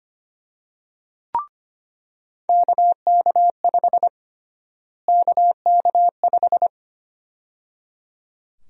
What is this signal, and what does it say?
Morse code, heard as a keyed tone, about 25 wpm, 700 Hz. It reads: KK5 KK5
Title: USA Call Sign Prefixes - Repeated 2x - 25wpm